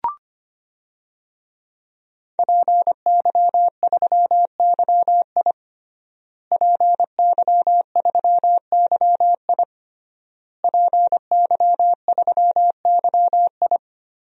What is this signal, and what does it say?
Morse code, heard as a keyed tone, about 25 wpm, 700 Hz. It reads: PY3YI PY3YI PY3YI
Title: DXCC-100 Call Signs - 5 Characters - Repeated 3x - 25wpm